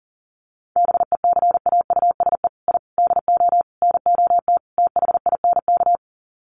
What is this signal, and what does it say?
Morse code, heard as a keyed tone, about 40 wpm, 700 Hz. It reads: BECAUSE I DO NOT THINK
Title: Sets of 5 Words Encore 40wpm